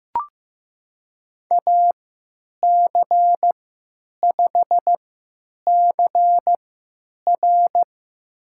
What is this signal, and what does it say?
Morse code, heard as a keyed tone, about 15 wpm, spaced as for 9 wpm, 700 Hz. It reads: AC5CR
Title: USA Call Signs - 3x Character Spacing - 15wpm